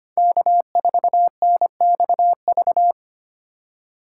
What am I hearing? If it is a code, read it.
Morse code, heard as a keyed tone, about 30 wpm, 700 Hz. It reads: K4NXV
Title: Speed Racing - USA 1x3 Call Signs 25wpm